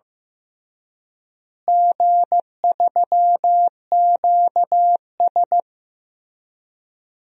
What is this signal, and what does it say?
Morse code, heard as a keyed tone, about 15 wpm, 700 Hz. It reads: G3QS
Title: DXCC-100 Call Signs - 4 Characters - Repeated 3x - 15wpm